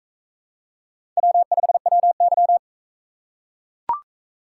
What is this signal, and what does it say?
Morse code, heard as a keyed tone, about 40 wpm, 700 Hz. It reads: W5WY